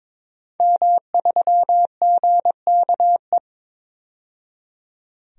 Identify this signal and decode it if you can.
Morse code, heard as a keyed tone, about 22 wpm, 700 Hz. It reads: M3GKE
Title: DXCC-100 Call Signs - 5 Characters - 22wpm